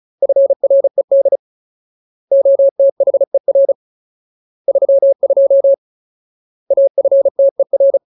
Morse code, heard as a keyed tone, about 34 wpm, 550 Hz. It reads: FRED OTHER 32 AFTER
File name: Ultimate Word List / 35WPM Ultimate